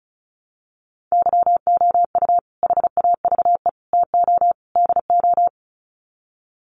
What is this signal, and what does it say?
Morse code, heard as a keyed tone, about 34 wpm, 700 Hz. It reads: YOU HAVE TO DO
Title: Sets of 4 Words 35wpm